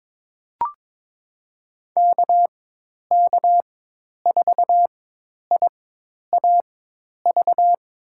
Morse code, heard as a keyed tone, about 22 wpm, spaced as for 10 wpm, 700 Hz. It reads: KK4IAV
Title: USA Call Signs - 6 Characters - 4x Character Spacing - 22wpm